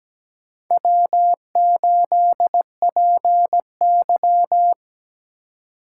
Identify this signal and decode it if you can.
Morse code, heard as a keyed tone, about 20 wpm, 700 Hz. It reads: W8PY
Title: Speed Racing - USA 1x2 Call Signs 17wpm